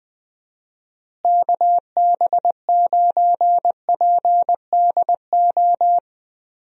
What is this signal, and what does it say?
Morse code, heard as a keyed tone, about 19 wpm, 700 Hz. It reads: KB9PDO
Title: USA Call Signs - 6 Characters - Repeated 3x - 20wpm